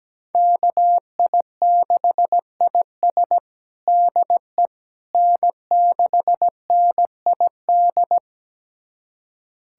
Morse code, heard as a keyed tone, about 17 wpm, 700 Hz. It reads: KI6IS DE N6NID
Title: USA Callsign DE Callsign - 5 Characters - 17wpm